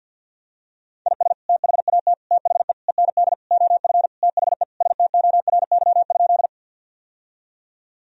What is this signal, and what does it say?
Morse code, heard as a keyed tone, about 52 wpm, 700 Hz. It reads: IS THAT THE END OF THE STORY?
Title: Sentences from Top 1000 Words - Review Part 1 - 50wpm